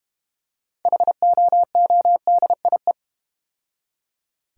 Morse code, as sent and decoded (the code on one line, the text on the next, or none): .... --- --- -.. .. .
HOODIE